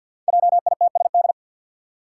Morse code, heard as a keyed tone, about 53 wpm, 700 Hz. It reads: JITSD
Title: Five Challenging Letter-Number Combinations 50wpm